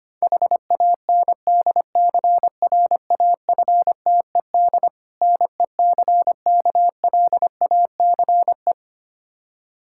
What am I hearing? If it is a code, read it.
Morse code, heard as a keyed tone, about 25 wpm, 700 Hz. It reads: HANDCRAFTED NECKLACE
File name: practice / 25wpm / Difficult Words with Hints - 25wpm